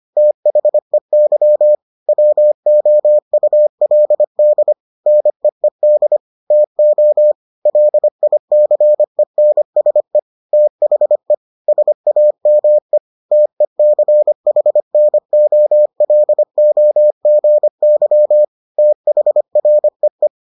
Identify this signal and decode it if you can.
Morse code, heard as a keyed tone, about 25 wpm, 600 Hz. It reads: THEY WOULD NEED TO LICENSE THE SAME TECHNOLOGY THREE